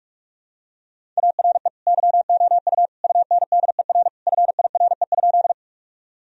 Morse code, heard as a keyed tone, about 41 wpm, 700 Hz. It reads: ARE YOU UNDER FIRE?
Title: Sentences from Top 1000 Words - Review Part 1 - 45wpm